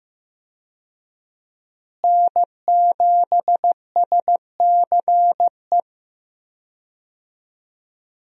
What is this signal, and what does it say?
Morse code, heard as a keyed tone, about 15 wpm, 700 Hz. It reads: N7SCE
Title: USA Call Signs - 5 Characters - Repeated 3x - 15wpm